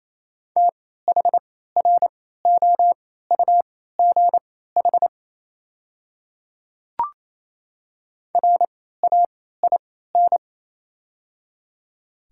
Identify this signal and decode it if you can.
Morse code, heard as a keyed tone, about 28 wpm, spaced as for 16 wpm, 700 Hz. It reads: THROUGH RAIN
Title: 3x Character Spacing - Top 500 Words - 28wpm